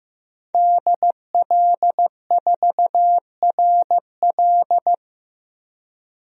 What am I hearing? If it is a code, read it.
Morse code, heard as a keyed tone, about 15 wpm, 700 Hz. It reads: DL4RL
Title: DXCC-100 Call Signs - 5 Characters - 15wpm